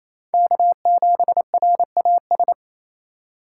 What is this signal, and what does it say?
Morse code, heard as a keyed tone, about 27 wpm, 690 Hz. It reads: K7RAS
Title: USA Call Signs - 3x Character Spacing - 28wpm